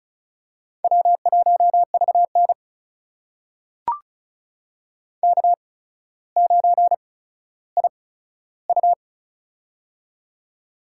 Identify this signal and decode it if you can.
Morse code, heard as a keyed tone, about 36 wpm, spaced as for 15 wpm, 700 Hz. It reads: W1VN K9IU
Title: USA Call Signs - 4 Characters - 8x Character Spacing - 35wpm